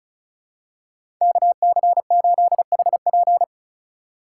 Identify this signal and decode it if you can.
Morse code, heard as a keyed tone, about 34 wpm, 700 Hz. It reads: KC8HP
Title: USA Call Signs - 5 Characters - 4x Character Spacing - 35wpm